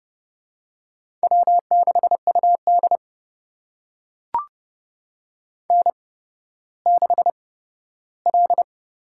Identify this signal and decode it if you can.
Morse code, heard as a keyed tone, about 31 wpm, spaced as for 15 wpm, 700 Hz. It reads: W6UD N6L
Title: USA Call Signs - 4 Characters - 8x Character Spacing - 30wpm